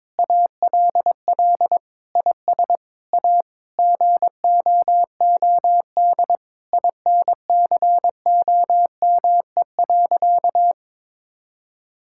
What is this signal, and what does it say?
Morse code, heard as a keyed tone, about 22 wpm, 700 Hz. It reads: ALL IS A GOOD INCOME.
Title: Sentences from Top 1000 Words - Review Part 6 - 22wpm